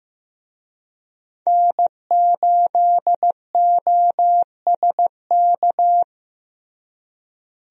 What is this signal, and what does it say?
Morse code, heard as a keyed tone, about 15 wpm, 700 Hz. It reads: N8OSK
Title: USA Call Signs - 5 Characters - Repeated 2x - 15wpm